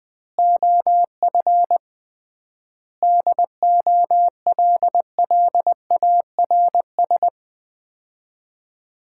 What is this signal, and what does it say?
Morse code, heard as a keyed tone, about 20 wpm, 700 Hz. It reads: OF DOLLARS
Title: 3x Word-Spacing - Sets of 5 Words - 20wpm